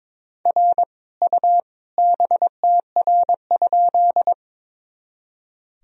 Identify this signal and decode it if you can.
Morse code, heard as a keyed tone, about 22 wpm, 700 Hz. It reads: R U BTR?
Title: Sentences from Top 400 Words and Abbreviations 22wpm